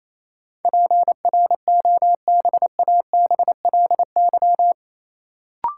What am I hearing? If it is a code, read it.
Morse code, heard as a keyed tone, about 28 wpm, 700 Hz. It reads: PROBABLY
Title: Sentences from Top 700 Words - New Word Drill - 28wpm